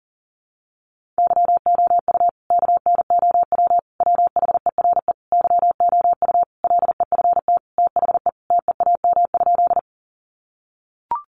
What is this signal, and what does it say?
Morse code, heard as a keyed tone, about 37 wpm, 700 Hz. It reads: YOU KNOW WHERE YOU LEFT THE TEAM?
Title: Sentences from Top 1500 Words 40wpm